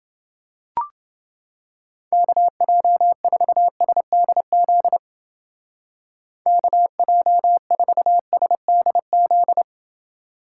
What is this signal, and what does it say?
Morse code, heard as a keyed tone, about 30 wpm, 700 Hz. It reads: KJ4SDZ KJ4SDZ